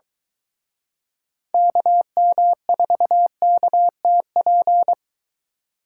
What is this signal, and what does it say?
Morse code, heard as a keyed tone, about 23 wpm, 700 Hz. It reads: KM4KTP